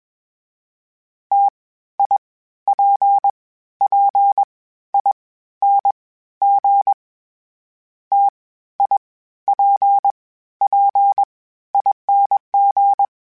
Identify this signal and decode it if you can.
Morse code, heard as a keyed tone, about 21 wpm, 800 Hz. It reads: T I P P I N G T I P P ING
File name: Variable Spacing / Variable Spacing-mp3s / T AOEs @20